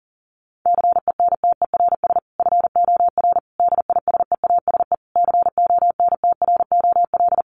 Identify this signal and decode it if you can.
Morse code, heard as a keyed tone, about 40 wpm, 700 Hz. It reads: CENTERS FOR DISEASE CONTROL
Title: Sets of 5 Words Encore2 40wpm